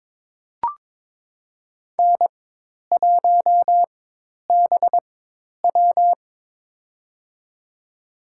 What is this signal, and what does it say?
Morse code, heard as a keyed tone, about 22 wpm, spaced as for 10 wpm, 700 Hz. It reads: N1BW
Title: USA Call Signs - 4 Characters - 4x Character Spacing - 22wpm